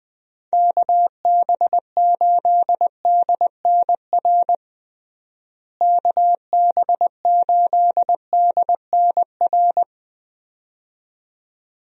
Morse code, heard as a keyed tone, about 21 wpm, 700 Hz. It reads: KB8DNR KB8DNR
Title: USA Call Signs - 6 Characters - Repeated 2x - 20wpm